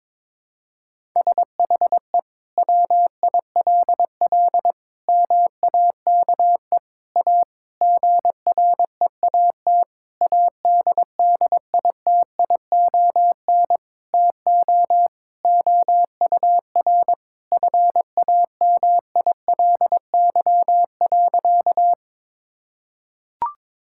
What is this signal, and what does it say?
Morse code, heard as a keyed tone, about 22 wpm, 700 Hz. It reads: SHE WILL MAKE A GREAT ADDITION TO OUR FAMILY.
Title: Sentences from Top 1000 Words - Review Part 6 - 22wpm